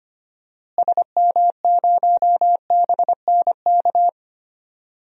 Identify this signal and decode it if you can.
Morse code, heard as a keyed tone, about 25 wpm, 700 Hz. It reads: SM0BNK
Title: DXCC-100 Call Signs - 6 Characters - Repeated 3x - 25wpm